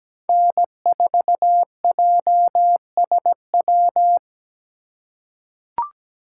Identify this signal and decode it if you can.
Morse code, heard as a keyed tone, about 17 wpm, 700 Hz. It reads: N4JSW
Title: USA Call Signs - 5 Characters - 4x Character Spacing - 17wpm